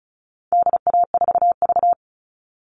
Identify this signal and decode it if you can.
Morse code, heard as a keyed tone, about 34 wpm, 700 Hz. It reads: DA4V
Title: All Cut Numbers - 4-Digit 35wpm